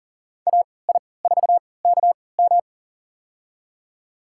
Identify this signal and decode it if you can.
Morse code, heard as a keyed tone, about 40 wpm, spaced as for 23 wpm, 700 Hz. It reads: AI4KM
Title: USA Call Signs - 5 Characters - 3x Character Spacing - 40wpm